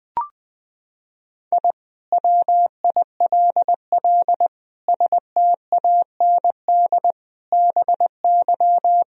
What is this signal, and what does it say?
Morse code, heard as a keyed tone, about 20 wpm, 700 Hz. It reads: I WILL STAND BY
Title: Sentences from Top 1000 Words - Review Part 1 - 20wpm